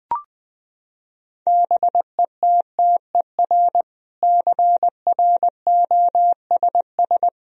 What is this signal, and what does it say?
Morse code, heard as a keyed tone, about 20 wpm, 700 Hz. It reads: BETTER CROSS
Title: Sentences from Top 1000 Words - Review Part 1 - 20wpm